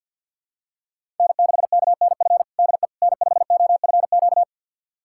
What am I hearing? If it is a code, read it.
Morse code, heard as a keyed tone, about 51 wpm, 700 Hz. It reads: N6KNF DE N5OFQ
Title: USA Callsign DE Callsign - 5 Characters - 50wpm